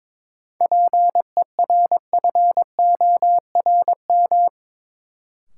Morse code, heard as a keyed tone, about 22 wpm, 700 Hz. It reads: PERFORM
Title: Sentences from Top 1000 Words - Review Part 5 - New Word Drill - 22wpm